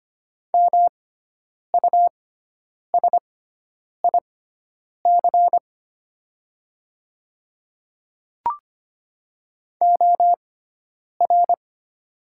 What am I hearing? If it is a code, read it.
Morse code, heard as a keyed tone, about 25 wpm, spaced as for 9 wpm, 700 Hz. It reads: MUSIC OR
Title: 6x Character Spacing - Top 500 Words - 25wpm